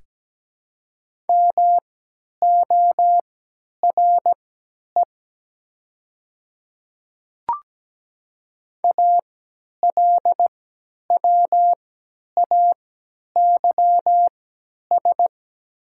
Morse code, heard as a keyed tone, about 17 wpm, spaced as for 10 wpm, 700 Hz. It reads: MORE ALWAYS
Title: 3x Character Spacing - Top 500 Words - 17wpm